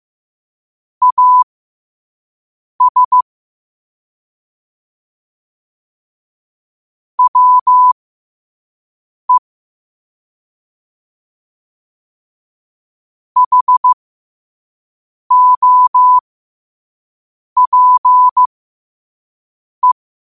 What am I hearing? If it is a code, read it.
Morse code, heard as a keyed tone, about 15 wpm, spaced as for 5 wpm, 1000 Hz. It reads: AS WE HOPE
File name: qotdfiles / 5wpm